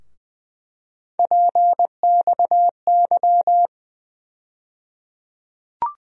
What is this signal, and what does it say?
Morse code, heard as a keyed tone, about 20 wpm, 700 Hz. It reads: PXY
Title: USA Call Sign Postfixes - 3 Letters 20wpm